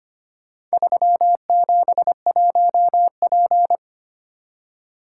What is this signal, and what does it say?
Morse code, heard as a keyed tone, about 25 wpm, 700 Hz. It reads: 371P
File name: practice / 25wpm / Four Letter-Number Combinations - 25wpm